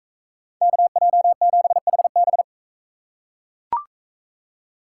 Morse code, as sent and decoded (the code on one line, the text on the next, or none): -.- .--- --... .... -...
KJ7HB